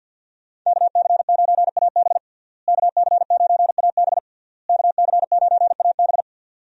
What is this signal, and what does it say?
Morse code, heard as a keyed tone, about 52 wpm, 700 Hz. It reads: KC9AB KC9AB KC9AB
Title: USA Call Signs - 5 Characters - Repeated 3x - 50wpm